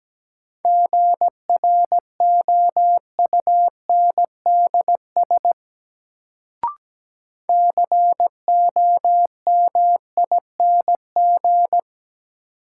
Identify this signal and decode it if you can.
Morse code, heard as a keyed tone, about 17 wpm, 700 Hz. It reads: GROUNDS COMING
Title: Sentences from Top 1300 Words - New Word Drill Alt - 17wpm